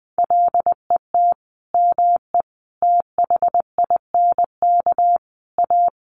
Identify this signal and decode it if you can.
Morse code, heard as a keyed tone, about 20 wpm, 700 Hz. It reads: LET ME THINK A
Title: Sentences from Top 1000 Words 20wpm